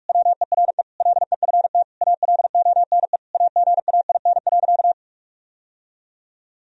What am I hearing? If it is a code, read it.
Morse code, heard as a keyed tone, about 46 wpm, 700 Hz. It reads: WERE LEFT ALONE AGAIN.
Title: Sentences from Top 1000 Words - Review Part 1 - 45wpm